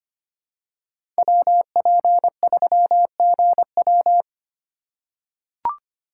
Morse code, heard as a keyed tone, about 25 wpm, 700 Hz. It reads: WP3GW
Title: USA Call Signs - 5 Characters - 6x Character Spacing - 25wpm